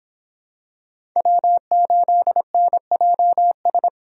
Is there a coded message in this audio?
Yes: Morse code from 1.2 to 3.9 s, heard as a keyed tone, about 25 wpm, 700 Hz. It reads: W8NJS